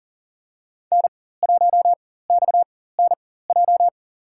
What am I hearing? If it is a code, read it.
Morse code, heard as a keyed tone, about 40 wpm, spaced as for 19 wpm, 700 Hz. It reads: N1XNJ
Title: USA Call Signs - 5 Characters - 4x Character Spacing - 40wpm